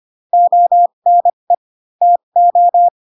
Morse code, heard as a keyed tone, about 25 wpm, spaced as for 22 wpm, 700 Hz. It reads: ONE TO
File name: Pinocchio / Pinocchio ch2 18 25 ik1hge